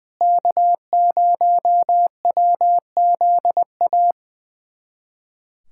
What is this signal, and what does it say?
Morse code, heard as a keyed tone, about 20 wpm, 700 Hz. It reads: K0WZA
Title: USA Callsign DE Callsign 20wpm